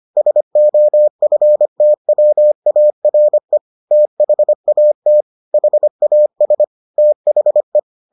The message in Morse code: SOFTWARE THAT HAS THE

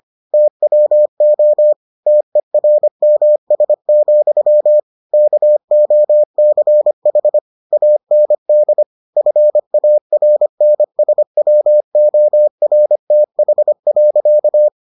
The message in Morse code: TWO TERMS, KOCH AND FARNSWORTH.

